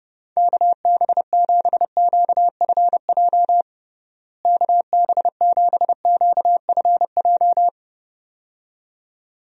Morse code, heard as a keyed tone, about 30 wpm, 700 Hz. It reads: KB7QFJ KB7QFJ
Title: USA Call Signs - 6 Characters - Repeated 2x - 30wpm